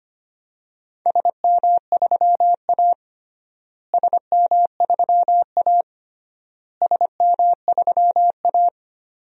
Morse code, heard as a keyed tone, about 25 wpm, 700 Hz. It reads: SM3A SM3A SM3A
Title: DXCC-100 Call Signs - 4 Characters - Repeated 3x - 25wpm